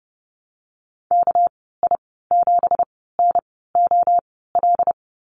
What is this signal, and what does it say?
Morse code, heard as a keyed tone, about 30 wpm, spaced as for 17 wpm, 700 Hz. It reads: KI7NOL
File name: Lesson 37g - Letter x - Call Signs 30wpm